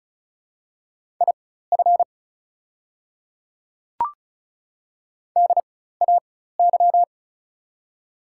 Morse code, heard as a keyed tone, about 36 wpm, spaced as for 16 wpm, 700 Hz. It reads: IF DAY
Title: 4x Character Spacing - Top 100 Words - 35wpm